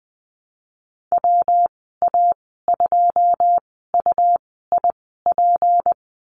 Morse code, heard as a keyed tone, about 20 wpm, spaced as for 15 wpm, 700 Hz. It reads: WA2UIP
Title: 173 Lesson 28g - Number 9 - Call Signs 20wpm